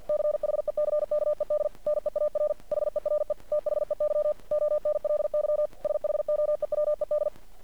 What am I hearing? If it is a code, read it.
Morse code, heard as a keyed tone, about 50 wpm, 600 Hz. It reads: CHECKED NEAR HERE THEY ONLY SHOEWED
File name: archived / CRSnet